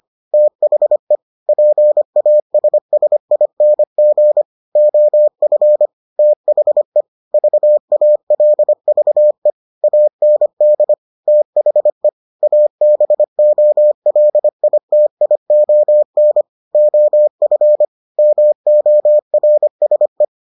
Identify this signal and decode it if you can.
Morse code, heard as a keyed tone, about 25 wpm, 600 Hz. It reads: THE PASSING OF THE VALVE AND THE ABOLITION OF MORSE